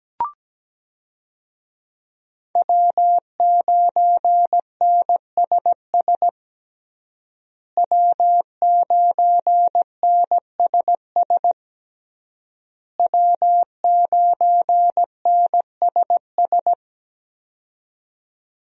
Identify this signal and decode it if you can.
Morse code, heard as a keyed tone, about 17 wpm, 700 Hz. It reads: W9NSS W9NSS W9NSS
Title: USA Call Signs - 5 Characters - Repeated 3x - 17wpm